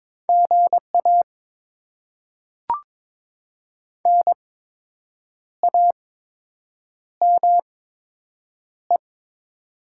Morse code, heard as a keyed tone, about 22 wpm, spaced as for 7 wpm, 700 Hz. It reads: GA NAME
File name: Top 20 QSO Elements - 8x Character Spacing - 22wpm